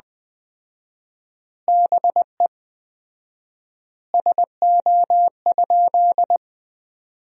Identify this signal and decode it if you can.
Morse code, heard as a keyed tone, about 21 wpm, 700 Hz. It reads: BE SO?
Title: 4x Word-Spacing - Sentences from Top 300 Words - 20wpm